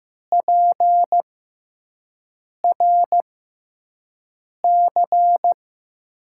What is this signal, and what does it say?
Morse code, heard as a keyed tone, about 15 wpm, 700 Hz. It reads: P R C
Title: USA Call Signs - 6x Character Spacing - 15wpm